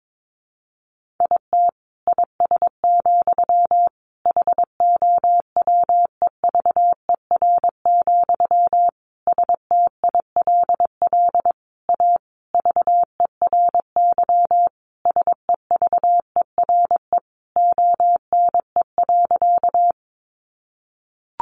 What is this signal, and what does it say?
Morse code, heard as a keyed tone, about 22 wpm, 700 Hz. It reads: IT IS, HOWEVER, STILL A VERY SEVERE ONE.
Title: Sentences from Top 1400 Words 22wpm